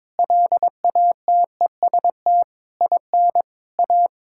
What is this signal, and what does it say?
Morse code, heard as a keyed tone, about 22 wpm, 700 Hz. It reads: LATEST IN A
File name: practice / 22wpm / Sentences with 8-Word Phrases - 22wpm